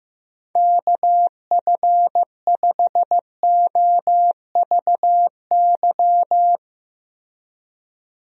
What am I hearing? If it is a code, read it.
Morse code, heard as a keyed tone, about 15 wpm, 700 Hz. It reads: KF5OVY
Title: USA Call Signs - 6 Characters - Repeated 2x - 15wpm